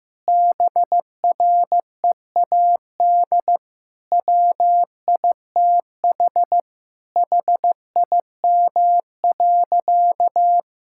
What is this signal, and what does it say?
Morse code, heard as a keyed tone, about 15 wpm, 700 Hz. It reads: BREAD WITH HIM.
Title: Sentences from Top 1600 Words 15wpm